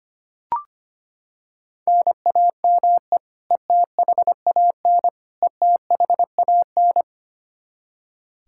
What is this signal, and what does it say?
Morse code, heard as a keyed tone, about 25 wpm, 700 Hz. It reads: NAME ETHAN ETHAN
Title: Practice QSO - Exchange 1 - Beginning - 25wpm